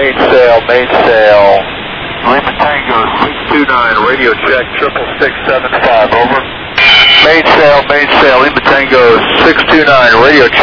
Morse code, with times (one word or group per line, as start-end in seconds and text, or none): none